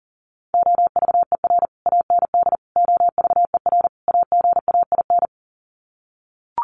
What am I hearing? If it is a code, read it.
Morse code, heard as a keyed tone, about 40 wpm, 700 Hz. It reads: OVER AND OVER AGAIN
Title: Sets of 4 Words 40wpm